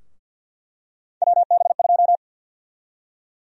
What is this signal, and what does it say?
Morse code, heard as a keyed tone, about 47 wpm, 700 Hz. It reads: WB2
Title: USA Call Sign Prefixes - Repeated 2x - 50wpm